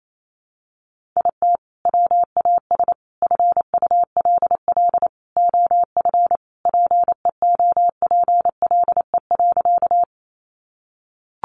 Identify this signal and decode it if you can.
Morse code, heard as a keyed tone, about 28 wpm, 700 Hz. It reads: IT WAS FULL OF PEOPLE.
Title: Sentences from Top 500 Words 28wpm